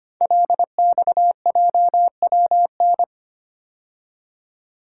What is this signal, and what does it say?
Morse code, heard as a keyed tone, about 25 wpm, 690 Hz. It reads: LXJWN
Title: Code Groups - 5 Letter:Numbers - 25wpm